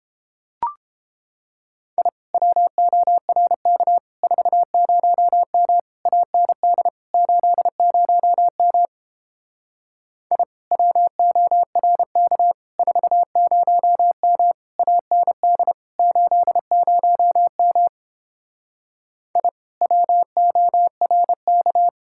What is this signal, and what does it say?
Morse code, heard as a keyed tone, about 29 wpm, 700 Hz. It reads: I WORK 40M AND 80M I WORK 40M AND 80M I WORK